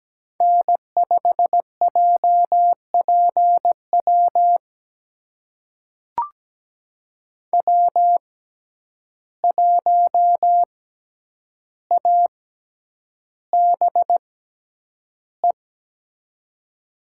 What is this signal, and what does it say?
Morse code, heard as a keyed tone, about 17 wpm, spaced as for 9 wpm, 700 Hz. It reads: N5JPW W1ABE
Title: USA Call Signs - 5 Characters - 6x Character Spacing - 17wpm